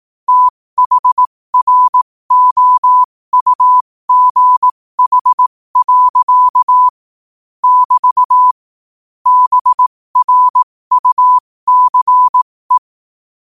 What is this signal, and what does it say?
Morse code, heard as a keyed tone, about 18 wpm, spaced as for 15 wpm, 1000 Hz. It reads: THROUGH. <BT> BRUCE
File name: qotdfiles / 15wpm